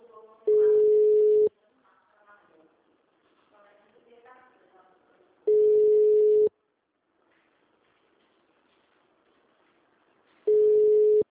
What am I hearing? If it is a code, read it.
Ringback tone (Europe, 425 Hz), with speech.